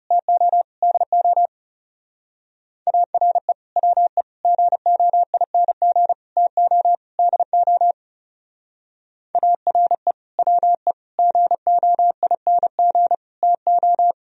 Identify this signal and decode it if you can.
Morse code, heard as a keyed tone, about 37 wpm, 700 Hz. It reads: TO DO ARE WE GOING TO DO ARE WE GOING TO
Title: Speed-Racing - Sets of 5 Words 30wpm